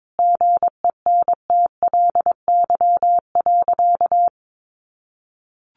Morse code, heard as a keyed tone, about 22 wpm, 700 Hz. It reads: GENTLY.
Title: Sentences from Top 1500 Words 22wpm